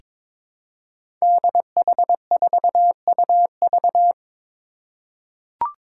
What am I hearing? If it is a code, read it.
Morse code, heard as a keyed tone, about 22 wpm, 700 Hz. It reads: DH4UV
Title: Five Challenging Letter-Number Combinations 22wpm